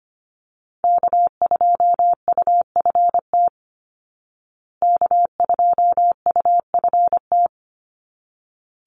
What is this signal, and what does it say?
Morse code, heard as a keyed tone, about 25 wpm, 700 Hz. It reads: K2UFT K2UFT
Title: Call Signs Repeated 3x - 25wpm